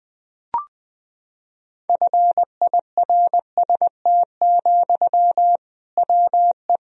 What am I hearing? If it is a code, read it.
Morse code, heard as a keyed tone, about 20 wpm, 700 Hz. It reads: FIRST, WE
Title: Sentences from Top 1000 Words - Review Part 1 - 20wpm